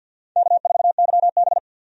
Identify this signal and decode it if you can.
Morse code, heard as a keyed tone, about 52 wpm, 700 Hz. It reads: K4YB
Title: USA Call Signs - 4 Characters - 50wpm